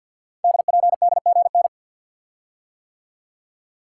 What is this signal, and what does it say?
Morse code, heard as a keyed tone, about 50 wpm, 700 Hz. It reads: DPDGN
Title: Code Groups - 5 Letters - 50wpm